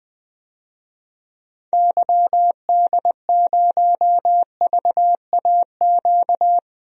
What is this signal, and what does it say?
Morse code, heard as a keyed tone, about 21 wpm, 700 Hz. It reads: YD0VAQ
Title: DXCC-100 Call Signs - 6 Characters - 20wpm